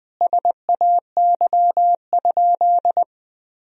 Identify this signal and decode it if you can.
Morse code, heard as a keyed tone, about 21 wpm, 700 Hz. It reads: SAY?
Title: Sentences from Top 200 Words and Abbreviations 20wpm